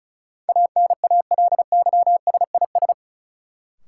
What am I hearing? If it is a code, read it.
Morse code, heard as a keyed tone, about 34 wpm, 700 Hz. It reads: ANALYSIS
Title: Sentences from Top 1000 Words - Review Part 5 - New Word Drill - 35wpm